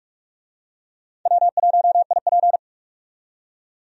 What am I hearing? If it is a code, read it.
Morse code, heard as a keyed tone, about 46 wpm, 700 Hz. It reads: W1IP
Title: USA Call Signs - 4 Characters - 45wpm